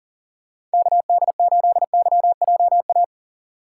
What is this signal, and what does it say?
Morse code, heard as a keyed tone, about 40 wpm, 700 Hz. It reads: KD8YJA
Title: USA Call Signs - 6 Characters - 3x Character Spacing - 40wpm